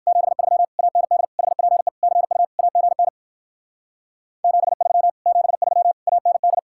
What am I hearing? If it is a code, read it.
Morse code, heard as a keyed tone, about 59 wpm, 700 Hz. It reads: 73 AND HPE CU AGN 73 73 AND